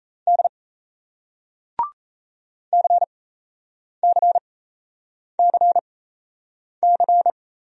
Morse code, heard as a keyed tone, about 38 wpm, spaced as for 8 wpm, 700 Hz. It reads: D CCCC